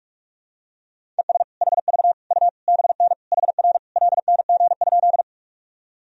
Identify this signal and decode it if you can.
Morse code, heard as a keyed tone, about 41 wpm, 700 Hz. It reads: ES HV U BN HR LNG?
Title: Sentences from Top 200 Words and Abbreviations 45wpm